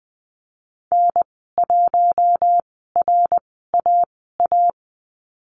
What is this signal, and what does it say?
Morse code, heard as a keyed tone, about 20 wpm, spaced as for 14 wpm, 700 Hz. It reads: N1RAA
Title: 076 Lesson 13g - Letter L - Call Signs 20wpm